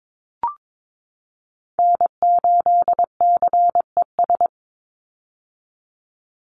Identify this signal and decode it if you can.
Morse code, heard as a keyed tone, about 22 wpm, 700 Hz. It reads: N8CES
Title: USA Call Signs 22wpm